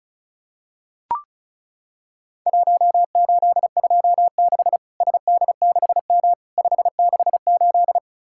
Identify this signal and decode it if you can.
Morse code, heard as a keyed tone, about 36 wpm, 700 Hz. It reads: 1826 SD6M 568
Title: Worked All Europe DX Contest - QTC Exchange - 35wpm